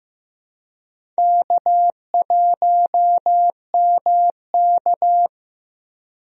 Morse code, heard as a keyed tone, about 15 wpm, 700 Hz. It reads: K1MK
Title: USA Call Signs - 4 Characters - Repeated 3x - 15wpm